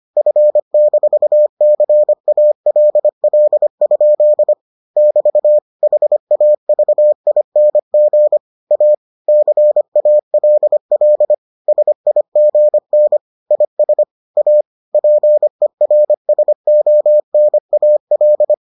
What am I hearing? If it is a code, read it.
Morse code, heard as a keyed tone, about 25 wpm, 600 Hz. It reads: F-CALL? <BT> HAVING A CALL SIGN IS A PERSONAL